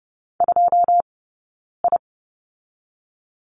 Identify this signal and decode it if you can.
Morse code, heard as a keyed tone, about 30 wpm, 700 Hz. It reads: 2 I
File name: Lesson 16d - Letter c - Two Characters 30wpm